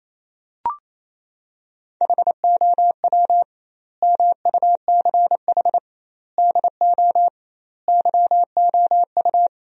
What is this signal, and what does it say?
Morse code, heard as a keyed tone, about 28 wpm, 700 Hz. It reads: HOW MUCH DO YOU
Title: 2x Word-Spacing - Sentences from Top 200 Words - 28wpm